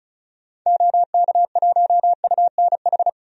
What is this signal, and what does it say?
Morse code, heard as a keyed tone, about 34 wpm, 700 Hz. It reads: OK1UNH
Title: DXCC-100 Call Signs - 6 Characters - 35wpm